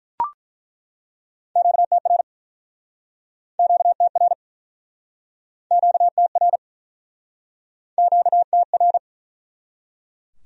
Morse code, heard as a keyed tone, about 47 wpm, 700 Hz. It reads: QTR QTR QTR QTR